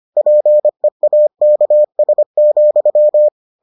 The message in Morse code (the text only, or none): PEAKS,